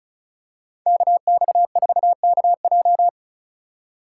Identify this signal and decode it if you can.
Morse code, heard as a keyed tone, about 34 wpm, 700 Hz. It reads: KX4KJ
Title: USA Call Signs - 5 Characters - 3x Character Spacing - 35wpm